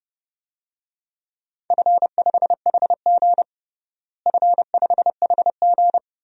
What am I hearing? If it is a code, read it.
Morse code, heard as a keyed tone, about 30 wpm, 700 Hz. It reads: F5HG F5HG
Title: DXCC-100 Call Signs - 4 Characters - Repeated 3x - 30wpm